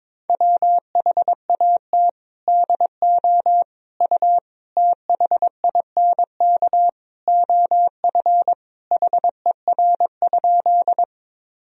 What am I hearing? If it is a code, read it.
Morse code, heard as a keyed tone, about 22 wpm, 700 Hz. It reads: WHAT DO U THINK OF HER?
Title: Sentences from Top 200 Words and Abbreviations 22wpm